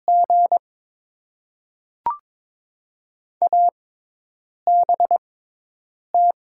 Morse code, heard as a keyed tone, about 22 wpm, 700 Hz. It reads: G A B T
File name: Top 90 QSO Elements - 6x Character Spacing - 22wpm